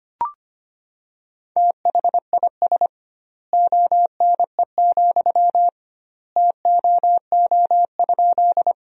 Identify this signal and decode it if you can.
Morse code, heard as a keyed tone, about 25 wpm, 700 Hz. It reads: THIS ONE, TOO?
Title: 2x Word-Spacing - Sentences from Top 200 Words - 25wpm